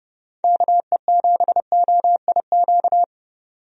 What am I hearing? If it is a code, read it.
Morse code, heard as a keyed tone, about 30 wpm, 700 Hz. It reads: KE7OIQ
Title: USA Call Signs - 6 Characters - 30wpm